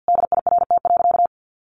Morse code, heard as a keyed tone, about 50 wpm, 700 Hz. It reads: DIRT.
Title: A Sound Of Thunder 50wpm